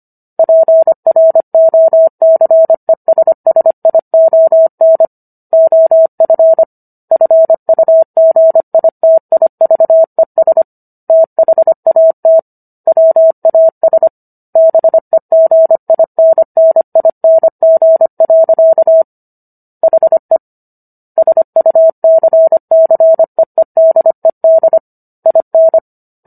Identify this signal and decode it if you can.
Morse code, heard as a keyed tone, about 25 wpm, 650 Hz. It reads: PROCESSION OF FUGITIVES THAT WAS BEGINNING. HE SUCCEEDED IN
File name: Book One / 16-Chapter 16 - 25 WPM